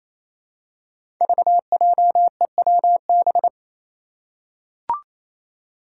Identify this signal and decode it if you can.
Morse code, heard as a keyed tone, about 28 wpm, 700 Hz. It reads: VJEWB
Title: Five Challenging Letter-Number Combinations 28wpm